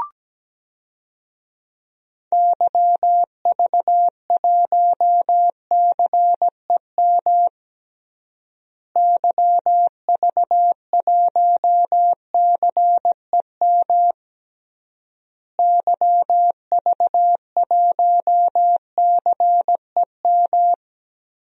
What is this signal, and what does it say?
Morse code, heard as a keyed tone, about 17 wpm, 700 Hz. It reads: YV1CEM YV1CEM YV1CEM
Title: DXCC-100 Call Signs - 6 Characters - Repeated 3x - 17wpm